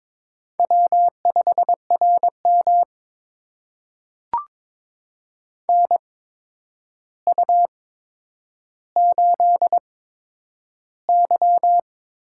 Morse code, heard as a keyed tone, about 22 wpm, spaced as for 9 wpm, 700 Hz. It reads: W5RM NU8Y
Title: USA Call Signs - 4 Characters - 8x Character Spacing - 22wpm